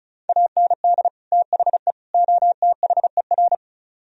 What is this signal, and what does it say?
Morse code, heard as a keyed tone, about 43 wpm, 700 Hz. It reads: AND THE OTHER
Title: Speed-Racing - Sets of 3 Words 35wpm